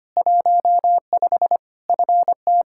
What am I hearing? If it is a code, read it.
Morse code, heard as a keyed tone, about 25 wpm, 700 Hz. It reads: 15 FT
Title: Practice QSO - Exchange 2 - Info - 25wpm